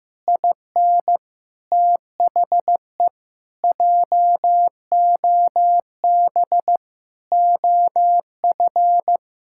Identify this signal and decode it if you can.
Morse code, heard as a keyed tone, about 15 wpm, 700 Hz. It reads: IN THE JOB OF
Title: Pangram Challenge v2 - 15wpm